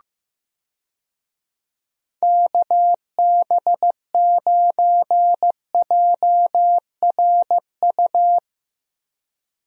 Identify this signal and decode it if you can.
Morse code, heard as a keyed tone, about 15 wpm, 700 Hz. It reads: KB9JRU
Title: USA Call Signs - 6 Characters - Repeated 2x - 15wpm